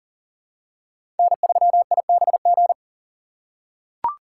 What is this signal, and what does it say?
Morse code, heard as a keyed tone, about 40 wpm, 700 Hz. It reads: N3IBG
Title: USA Call Signs - 2x Character Spacing - 40wpm